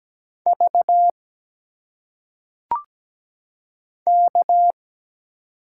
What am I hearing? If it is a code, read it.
Morse code, heard as a keyed tone, about 17 wpm, 700 Hz. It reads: V K
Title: Single Letters - Phonetic Pronunciation 17wpm